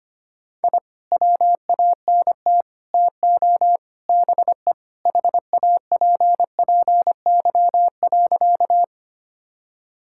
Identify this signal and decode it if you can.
Morse code, heard as a keyed tone, about 25 wpm, 700 Hz. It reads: I WANT TO BE HAPPY.
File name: practice / 25wpm / Sentences with 4-Word Phrases - 25wpm